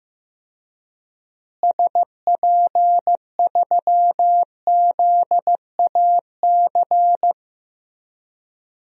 Morse code, heard as a keyed tone, about 15 wpm, 700 Hz. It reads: SP3ZAC
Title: DXCC-100 Call Signs - 6 Characters - Repeated 3x - 15wpm